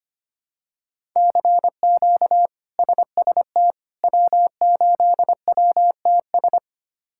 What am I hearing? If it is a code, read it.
Morse code, heard as a keyed tone, about 25 wpm, 700 Hz. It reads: CQ SST W8WTS
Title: K1USN SST QSO Practice - 25wpm